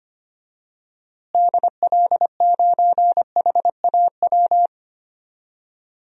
Morse code, heard as a keyed tone, about 25 wpm, 700 Hz. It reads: DL9HAW
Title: DXCC-100 Call Signs - 6 Characters - Repeated 3x - 25wpm